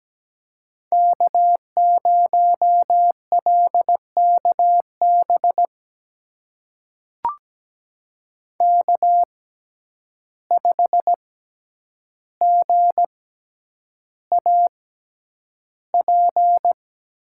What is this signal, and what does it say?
Morse code, heard as a keyed tone, about 17 wpm, spaced as for 9 wpm, 700 Hz. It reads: K0LKB K5GAP
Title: USA Call Signs - 5 Characters - 6x Character Spacing - 17wpm